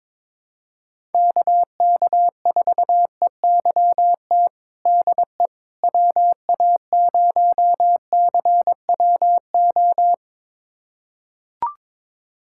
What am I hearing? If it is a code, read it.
Morse code, heard as a keyed tone, about 22 wpm, 700 Hz. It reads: KK4EYT DE WA0CWO
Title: USA Callsign DE Callsign - 6 Characters - 22wpm